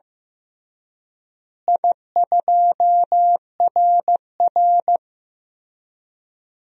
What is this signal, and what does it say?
Morse code, heard as a keyed tone, about 15 wpm, 700 Hz. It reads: I2RR
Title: DXCC-100 Call Signs - 4 Characters - Repeated 3x - 15wpm